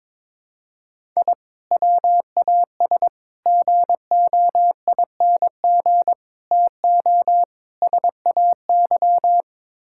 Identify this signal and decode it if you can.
Morse code, heard as a keyed tone, about 22 wpm, 700 Hz. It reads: I WAS GOING TO SAY
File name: practice / 22wpm / Build A Phrase - 5-Word Phrases - 22wpm